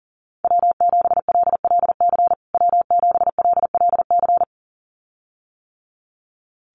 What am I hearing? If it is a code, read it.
Morse code, heard as a keyed tone, about 40 wpm, 700 Hz. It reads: W7LLC W7LLC
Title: USA Call Signs Repeated 2x 40wpm